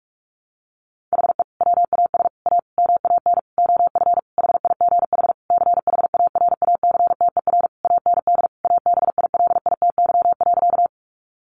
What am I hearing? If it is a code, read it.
Morse code, heard as a keyed tone, about 44 wpm, 700 Hz. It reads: HE WAS A MAN OF HIGH CHARACTER AND ABILITY.
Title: Sentences from Top 1200 Words 45wpm